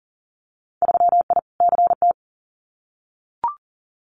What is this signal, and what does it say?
Morse code, heard as a keyed tone, about 40 wpm, 700 Hz. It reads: 3I CT
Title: Winter Field Day Exchanges 40wpm